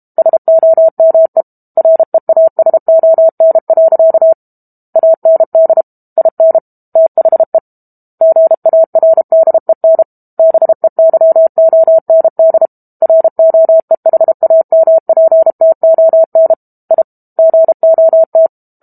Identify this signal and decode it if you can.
Morse code, heard as a keyed tone, about 32 wpm, 650 Hz. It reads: SOME REASON. AND IN THE GARDEN BEYOND ROEHAMPTON I GOT
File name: Book Two / 06-Chapter 6 - 32 WPM